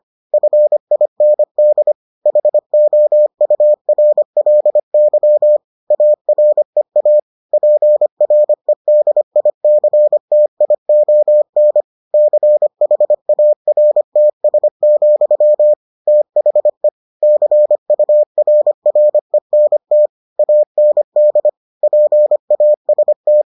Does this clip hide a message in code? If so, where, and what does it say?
Yes: Morse code from 0.3 to 23.4 s, heard as a keyed tone, about 25 wpm, 600 Hz. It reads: FIND HOURLY AREA PREDICTION CHARTS, THE CURRENT AND PAST